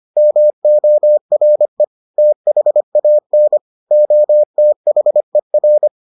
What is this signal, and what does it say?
Morse code, heard as a keyed tone, about 25 wpm, 600 Hz. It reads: MORE THAN OTHER